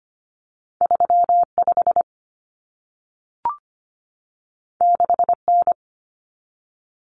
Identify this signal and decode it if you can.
Morse code, heard as a keyed tone, about 25 wpm, 700 Hz. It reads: 35 6N
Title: Common Cut Numbers - 2-Digit 25wpm